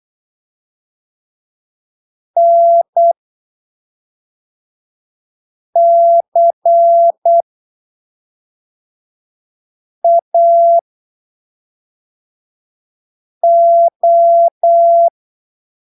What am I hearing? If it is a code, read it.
Morse code, heard as a keyed tone, about 8 wpm, spaced as for 3 wpm, 680 Hz. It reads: NCAO
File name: arquivos / texto / texto 17 08ppm